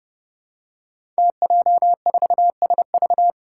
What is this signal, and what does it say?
Morse code, heard as a keyed tone, about 30 wpm, 700 Hz. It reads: TJ4SV
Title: Five Challenging Letter-Number Combinations 30wpm